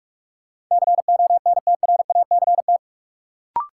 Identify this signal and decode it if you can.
Morse code, heard as a keyed tone, about 46 wpm, 700 Hz. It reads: CONTRACT
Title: Sentences from Top 1400 Words - New Word Drill - 45wpm